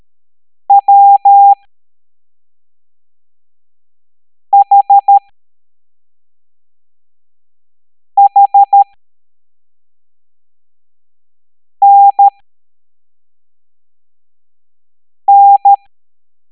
Morse code, heard as a keyed tone, about 13 wpm, spaced as for 3 wpm, 800 Hz. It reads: WHHNN